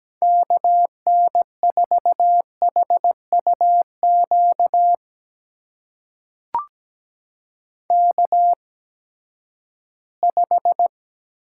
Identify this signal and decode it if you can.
Morse code, heard as a keyed tone, about 17 wpm, spaced as for 12 wpm, 700 Hz. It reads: KN4HUQ K5
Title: USA Call Signs - 8x Character Spacing - 17wpm